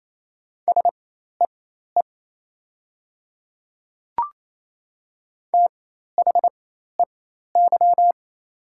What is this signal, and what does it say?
Morse code, heard as a keyed tone, about 28 wpm, spaced as for 13 wpm, 700 Hz. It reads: SEE THEY
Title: 4x Character Spacing - Top 200 Words - 28wpm